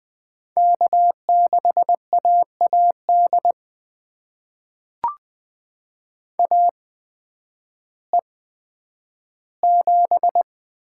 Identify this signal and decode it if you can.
Morse code, heard as a keyed tone, about 20 wpm, spaced as for 11 wpm, 700 Hz. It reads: K6AAD AE7
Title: USA Call Signs - 5 Characters - 8x Character Spacing - 20wpm